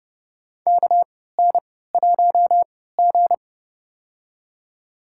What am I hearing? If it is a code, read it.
Morse code, heard as a keyed tone, about 30 wpm, spaced as for 17 wpm, 700 Hz. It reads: KN1G
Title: USA Call Signs - 4 Characters - 3x Character Spacing - 30wpm